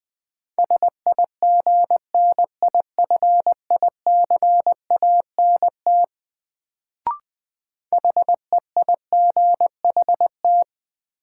Morse code, heard as a keyed tone, about 21 wpm, 700 Hz. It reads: SIGNIFICANT HEIGHT
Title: Sentences from Top 1600 Words - New Word Drill Alt - 20wpm